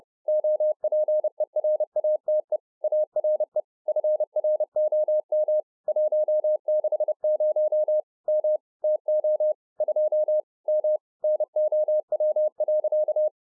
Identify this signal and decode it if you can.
Morse code, heard as a keyed tone, about 30 wpm, 610 Hz. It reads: OPERATE ARE FROM 160 M TO 2 M NOW.